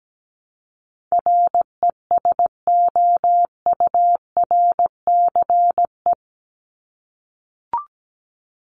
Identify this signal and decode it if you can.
Morse code, heard as a keyed tone, about 17 wpm, 700 Hz. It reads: RESOURCE
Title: ES:S Suffix - Top 100 Words with Base Words 17wpm